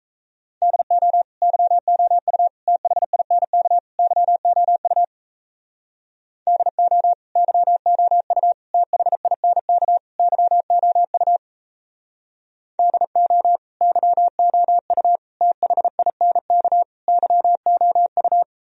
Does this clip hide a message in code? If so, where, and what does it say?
Yes: Morse code from 0.6 to 18.4 s, heard as a keyed tone, about 37 wpm, 700 Hz. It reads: DO YOU THINK YOU DO YOU THINK YOU DO YOU THINK YOU